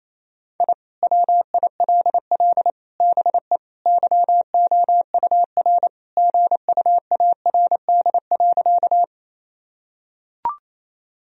Morse code, heard as a keyed tone, about 28 wpm, 700 Hz. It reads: I WILL BE YOUR GUARD.
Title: Sentences from Top 1000 Words - Review Part 4 - 28wpm